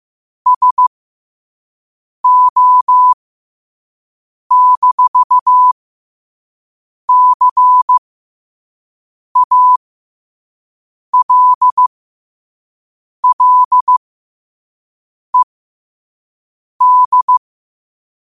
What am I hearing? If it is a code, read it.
Morse code, heard as a keyed tone, about 15 wpm, spaced as for 5 wpm, 1000 Hz. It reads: SO-CALLED
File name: qotdfiles / 5wpm